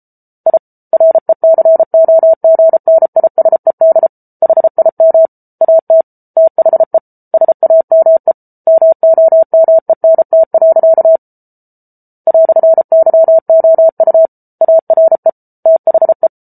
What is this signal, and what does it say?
Morse code, heard as a keyed tone, about 34 wpm, 650 Hz. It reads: I RECOGNISED HIM AT THE SAME MOMENT. "YOU ARE THE